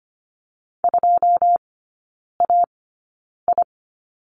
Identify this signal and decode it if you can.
Morse code, heard as a keyed tone, about 25 wpm, spaced as for 9 wpm, 690 Hz. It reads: 2AI
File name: Lesson 14e - Number 2 - Three Characters 25wpm